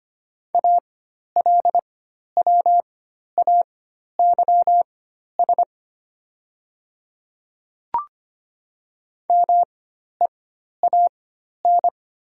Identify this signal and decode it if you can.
Morse code, heard as a keyed tone, about 25 wpm, spaced as for 12 wpm, 700 Hz. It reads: ALWAYS MEAN
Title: 4x Character Spacing - Top 300 Words - 25wpm